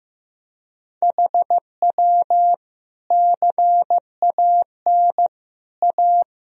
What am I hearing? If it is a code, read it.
Morse code, heard as a keyed tone, about 15 wpm, 700 Hz. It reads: HW CAN A
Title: Sentences from Top 500 Words and Abbreviations 15wpm